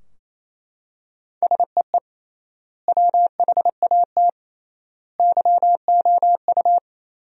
Morse code, heard as a keyed tone, about 28 wpm, 700 Hz. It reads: SEE WHAT YOU
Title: 3x Word-Spacing - Sentences from Top 500 Words - 28wpm